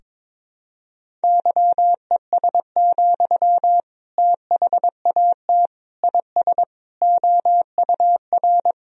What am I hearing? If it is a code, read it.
Morse code, heard as a keyed tone, about 22 wpm, 700 Hz. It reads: YES, THAT IS OUR
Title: Sentences from Top 1000 Words - Review Part 3 - 22wpm